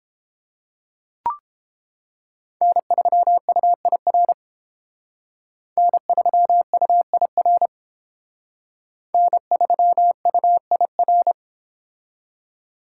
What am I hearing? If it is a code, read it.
Morse code, heard as a keyed tone, about 30 wpm, 700 Hz. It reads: N3UIR N3UIR N3UIR